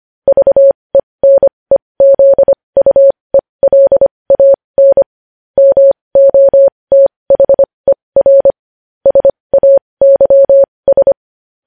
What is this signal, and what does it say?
Morse code, heard as a keyed tone, about 25 wpm, spaced as for 20 wpm, 550 Hz. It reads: VENEZUELAN MOTHER SAYS
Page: news-headlines-in-morse-code-at-20-wpm